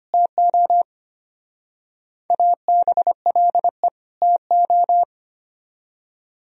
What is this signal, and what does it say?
Morse code, heard as a keyed tone, about 32 wpm, 700 Hz. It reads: TO ABLE TO